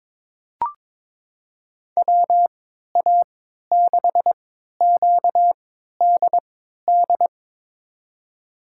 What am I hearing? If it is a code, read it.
Morse code, heard as a keyed tone, about 22 wpm, spaced as for 13 wpm, 700 Hz. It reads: WA6QDD
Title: USA Call Signs - 6 Characters - 3x Character Spacing - 22wpm